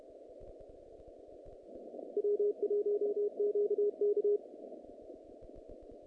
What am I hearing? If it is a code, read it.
Morse code, heard as a keyed tone, about 31 wpm, 410 Hz. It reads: W1QK
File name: contests / naqp-cw-k5tr-op-k5pi- -2200